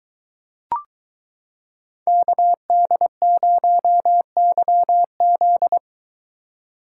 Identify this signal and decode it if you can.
Morse code, heard as a keyed tone, about 23 wpm, 700 Hz. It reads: KD0YZ